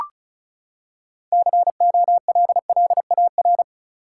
Morse code, heard as a keyed tone, about 36 wpm, 700 Hz. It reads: COLLAR
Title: Six Letter Words - 35wpm